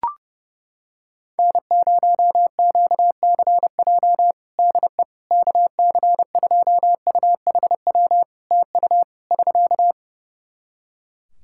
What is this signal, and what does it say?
Morse code, heard as a keyed tone, about 30 wpm, 700 Hz. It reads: N0QCJ DE KC2UHW TU <SK>
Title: K1USN SST QSO Practice 30wpm